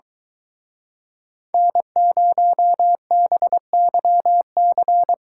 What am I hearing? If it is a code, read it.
Morse code, heard as a keyed tone, about 24 wpm, 700 Hz. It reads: N0BYC